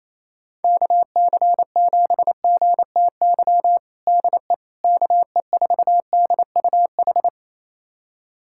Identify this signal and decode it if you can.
Morse code, heard as a keyed tone, about 27 wpm, 700 Hz. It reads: KC7GTY DE KE4DUH
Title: USA Callsign DE Callsign - 6 Characters - 28wpm